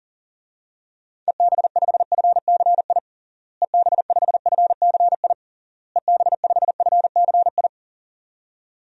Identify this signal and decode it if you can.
Morse code, heard as a keyed tone, about 40 wpm, 700 Hz. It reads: EB5FCI EB5FCI EB5FCI
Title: DXCC-100 Call Signs - 6 Characters - Repeated 3x - 40wpm